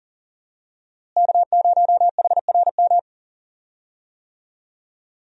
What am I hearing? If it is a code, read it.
Morse code, heard as a keyed tone, about 40 wpm, 700 Hz. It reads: K0HRM
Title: USA Call Signs - 5 Characters - Repeated 2x - 40wpm